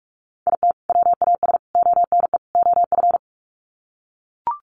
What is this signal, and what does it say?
Morse code, heard as a keyed tone, about 44 wpm, 700 Hz. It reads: IT WAS ONE OF
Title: Sets of 4 Words 45wpm